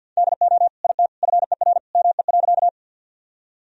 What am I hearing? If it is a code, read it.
Morse code, heard as a keyed tone, about 46 wpm, 700 Hz. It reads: DO IT FER ME.
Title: Sentences from Top 200 Words and Abbreviations 50wpm